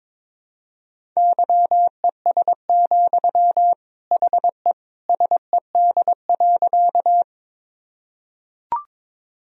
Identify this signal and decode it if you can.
Morse code, heard as a keyed tone, about 22 wpm, 700 Hz. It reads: YES, HE SED.
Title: Sentences from Top 500 Words and Abbreviations 22wpm